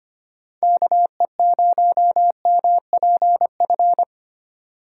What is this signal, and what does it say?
Morse code, heard as a keyed tone, about 25 wpm, 700 Hz. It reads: KE0MPF
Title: USA Call Signs - 6 Characters - 2x Character Spacing - 25wpm